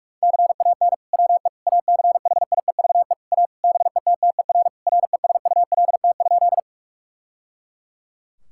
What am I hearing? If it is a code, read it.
Morse code, heard as a keyed tone, about 44 wpm, 700 Hz. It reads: CAN WE ACHIEVE A BETTER RESULT?
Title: Sentences from Top 1000 Words - Review Part 6 - 45wpm